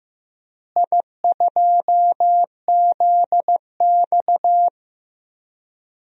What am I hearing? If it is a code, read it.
Morse code, heard as a keyed tone, about 15 wpm, 700 Hz. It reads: I2ZX
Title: DXCC-100 Call Signs - 4 Characters - Repeated 2x - 15wpm